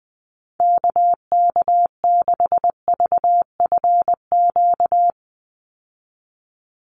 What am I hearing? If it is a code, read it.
Morse code, heard as a keyed tone, about 20 wpm, 700 Hz. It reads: KK6VFQ
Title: USA Call Signs 20wpm